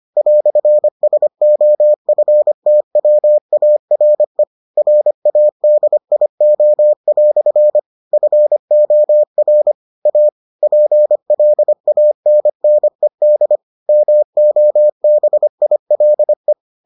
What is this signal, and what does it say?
Morse code, heard as a keyed tone, about 25 wpm, 600 Hz. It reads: "SOFTWARE RADIO" FOR A PLANNED MOBILE